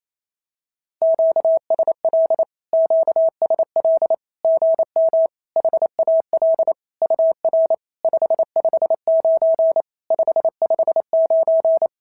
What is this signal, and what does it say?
Morse code, heard as a keyed tone, about 28 wpm, 650 Hz. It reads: QSL QSL GM HAL UR 559 559
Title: SOTA QSO Practice 28wpm